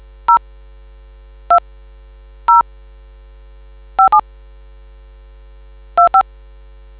DTMF, heard as telephone touch tones, tones of about 110 ms. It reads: 0 2 0 57 25